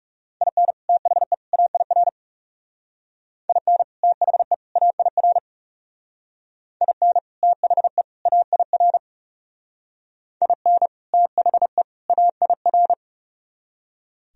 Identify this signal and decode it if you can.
Morse code, heard as a keyed tone, about 37 wpm, 700 Hz. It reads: IN THE AIR IN THE AIR IN THE AIR IN THE AIR
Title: Speed-Racing - Sets of 3 Words 30wpm